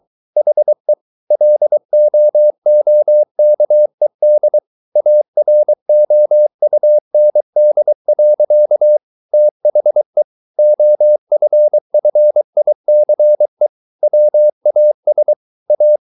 Morse code, heard as a keyed tone, about 23 wpm, 600 Hz. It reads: HE LOOKED AROUND. THE OFFICE WAS A